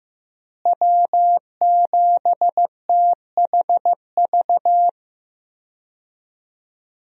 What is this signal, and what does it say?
Morse code, heard as a keyed tone, about 15 wpm, 700 Hz. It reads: W7THV
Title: USA Call Signs - 5 Characters - Repeated 2x - 15wpm